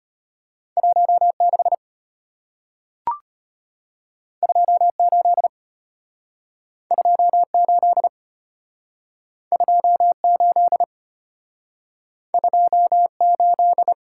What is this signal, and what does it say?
Morse code, heard as a keyed tone, about 33 wpm, 700 Hz. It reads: 1 6 2 8 2 8 2 8 2 8